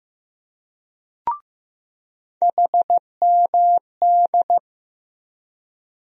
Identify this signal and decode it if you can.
Morse code, heard as a keyed tone, about 16 wpm, 700 Hz. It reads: HMD
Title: USA Call Sign Postfixes - 3 Letters 15wpm